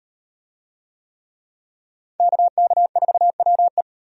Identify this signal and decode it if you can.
Morse code, heard as a keyed tone, about 38 wpm, 700 Hz. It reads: KK4WE